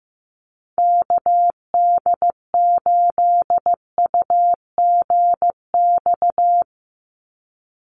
Morse code, heard as a keyed tone, about 15 wpm, 700 Hz. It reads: KD8UGX
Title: USA Call Signs 15wpm